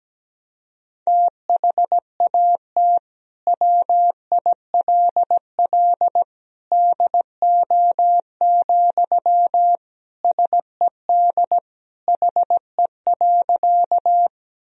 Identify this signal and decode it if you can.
Morse code, heard as a keyed tone, about 17 wpm, 700 Hz. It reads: THAT WILL DO, SED HE.
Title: Sentences from Top 100 Words and Abbreviations 17wpm